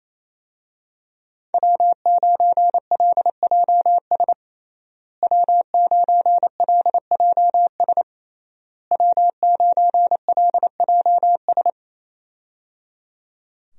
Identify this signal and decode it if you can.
Morse code, heard as a keyed tone, about 27 wpm, 700 Hz. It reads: W9LJS W9LJS W9LJS
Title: USA Call Signs - 5 Characters - Repeated 3x - 28wpm